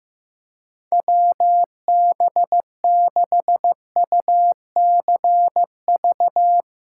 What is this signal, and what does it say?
Morse code, heard as a keyed tone, about 15 wpm, 700 Hz. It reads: WB6UCV
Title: USA Call Signs - 6x Character Spacing - 15wpm